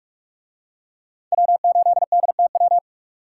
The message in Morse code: W8DTW

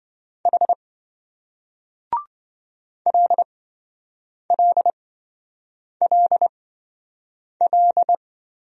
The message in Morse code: H LLLL

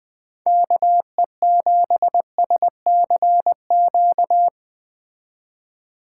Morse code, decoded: KE7SCQ

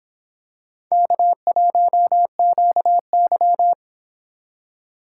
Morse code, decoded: K1QY

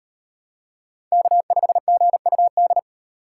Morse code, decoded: K5GUD

